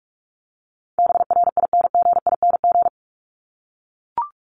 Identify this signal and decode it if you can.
Morse code, heard as a keyed tone, about 44 wpm, 700 Hz. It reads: BRINGING
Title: ING Suffix - Top 100 Words with Base Words 45wpm